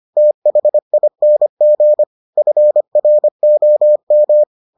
Morse code, heard as a keyed tone, about 25 wpm, 600 Hz. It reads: THING FROM